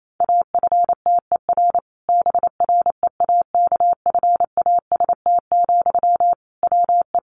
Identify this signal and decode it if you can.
Morse code, heard as a keyed tone, about 28 wpm, 700 Hz. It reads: AFTER BREAKFAST, WE
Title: Sentences from Top 1500 Words 28wpm